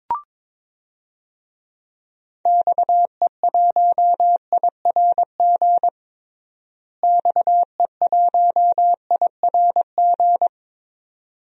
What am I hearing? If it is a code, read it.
Morse code, heard as a keyed tone, about 22 wpm, 700 Hz. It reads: XE1IRG XE1IRG
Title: DXCC-100 Call Signs - 6 Characters - Repeated 2x - 22wpm